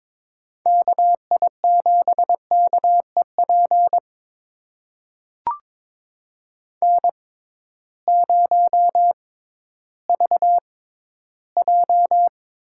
Morse code, heard as a keyed tone, about 22 wpm, spaced as for 13 wpm, 700 Hz. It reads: KI7KEP N0VJ
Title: USA Call Signs - 6x Character Spacing - 22wpm